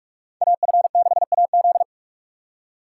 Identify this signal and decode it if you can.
Morse code, heard as a keyed tone, about 45 wpm, 700 Hz. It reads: AF6AZ